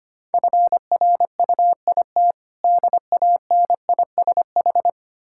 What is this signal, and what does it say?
Morse code, heard as a keyed tone, about 25 wpm, 690 Hz. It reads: FRUIT DANISH
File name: practice / 25wpm / Difficult Words with Hints - 25wpm